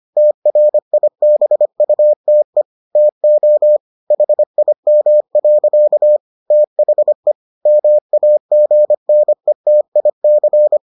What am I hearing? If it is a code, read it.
Morse code, heard as a keyed tone, about 25 wpm, 600 Hz. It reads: TRIBUTE TO HIM. THE MAGNETIC